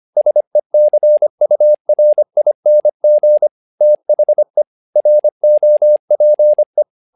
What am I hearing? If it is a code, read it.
Morse code, heard as a keyed tone, about 25 wpm, 600 Hz. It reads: SECURING THE ROPE
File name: learning center / tower trouble / tower trouble 25wpm